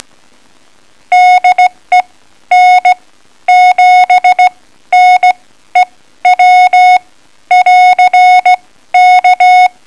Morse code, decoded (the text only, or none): DE N7NEW <AR> K